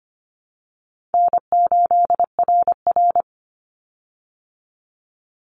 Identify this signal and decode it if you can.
Morse code, heard as a keyed tone, about 25 wpm, 700 Hz. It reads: N8RR
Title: Call Signs - 25wpm